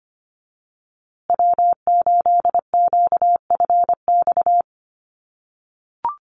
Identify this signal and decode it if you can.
Morse code, heard as a keyed tone, about 25 wpm, 700 Hz. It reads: W8QFX
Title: USA Call Signs Repeated 2x 25wpm